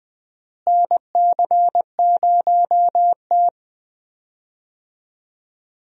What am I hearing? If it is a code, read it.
Morse code, heard as a keyed tone, about 21 wpm, 700 Hz. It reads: NC0T
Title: USA Call Signs - 4 Characters - Repeated 3x - 20wpm